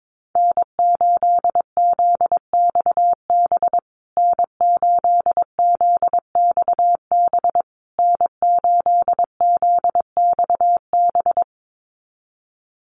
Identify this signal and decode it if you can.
Morse code, heard as a keyed tone, about 22 wpm, 700 Hz. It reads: N8ZXB N8ZXB N8ZXB
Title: USA Call Signs Repeated 3x 22wpm